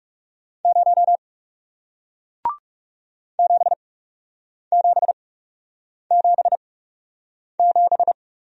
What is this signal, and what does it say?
Morse code, heard as a keyed tone, about 40 wpm, spaced as for 8 wpm, 700 Hz. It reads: 0 7777